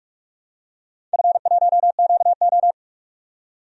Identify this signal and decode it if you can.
Morse code, heard as a keyed tone, about 45 wpm, 700 Hz. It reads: F1QO
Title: DXCC-100 Call Signs - 4 Characters - 45wpm